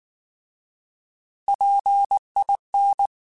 Morse code, heard as a keyed tone, about 19 wpm, 800 Hz. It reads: PIN